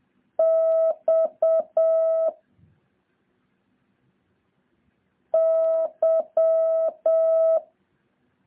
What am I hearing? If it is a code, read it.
Morse code, heard as a keyed tone, about 7 wpm, 650 Hz. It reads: X Y
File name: jh8cbh / 2023cw